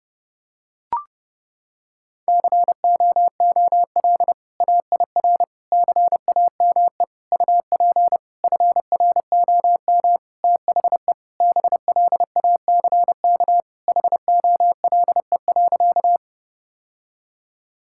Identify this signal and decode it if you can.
Morse code, heard as a keyed tone, about 30 wpm, 700 Hz. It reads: COOL AIR CAME UP FROM THE BLACK HOLE.
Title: Sentences from Top 1600 Words 30wpm